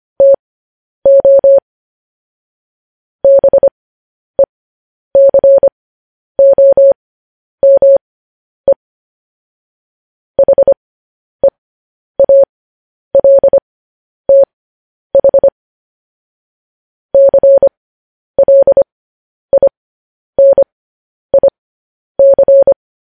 Morse code, heard as a keyed tone, about 25 wpm, spaced as for 10 wpm, 550 Hz. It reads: TO BECOME HEALTH CLINIC
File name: podcast / News headlines-1x-25x10